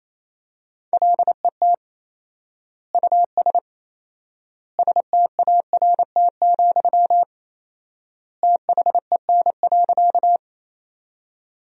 Morse code, heard as a keyed tone, about 28 wpm, 700 Hz. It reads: LET US START, THEN.
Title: 4x Word-Spacing - Sentences from Top 300 Words - 28wpm